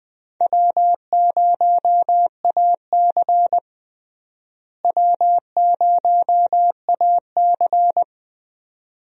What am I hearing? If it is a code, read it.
Morse code, heard as a keyed tone, about 20 wpm, 700 Hz. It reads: W0AC W0AC
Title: USA Call Signs - 4 Characters - Repeated 2x - 20wpm